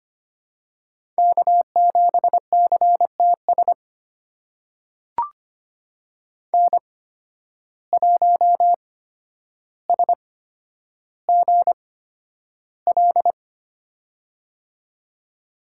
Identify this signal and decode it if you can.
Morse code, heard as a keyed tone, about 25 wpm, spaced as for 11 wpm, 700 Hz. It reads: K7CTS N1SGL
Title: USA Call Signs - 5 Characters - 8x Character Spacing - 25wpm